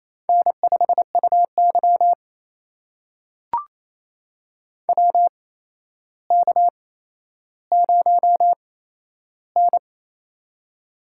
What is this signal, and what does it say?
Morse code, heard as a keyed tone, about 28 wpm, spaced as for 12 wpm, 700 Hz. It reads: N5UY WK0N
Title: USA Call Signs - 4 Characters - 8x Character Spacing - 28wpm